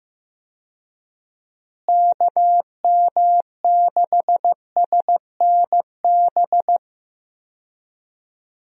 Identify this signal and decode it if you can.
Morse code, heard as a keyed tone, about 15 wpm, 700 Hz. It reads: KM6SNB
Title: USA Call Signs - 6 Characters - Repeated 2x - 15wpm